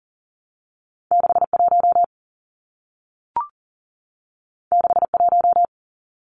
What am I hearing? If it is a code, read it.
Morse code, heard as a keyed tone, about 40 wpm, 700 Hz. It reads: 61 61
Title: Numbers - 2-Digit 40wpm